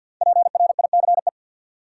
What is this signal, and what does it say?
Morse code, heard as a keyed tone, about 52 wpm, 700 Hz. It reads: PRICE
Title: Sentences from Top 600 Words - New Word Drill - 50wpm